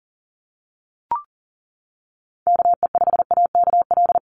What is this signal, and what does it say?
Morse code, heard as a keyed tone, about 40 wpm, 700 Hz. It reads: KE5AKL
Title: NA SOTA Activator Callsigns 40wpm